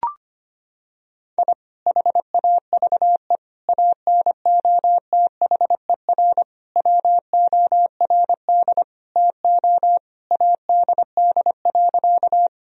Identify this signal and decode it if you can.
Morse code, heard as a keyed tone, about 25 wpm, 700 Hz. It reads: I HAVE ANOTHER WORD TO ADD.
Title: Sentences from Top 1000 Words - Review Part 1 - 25wpm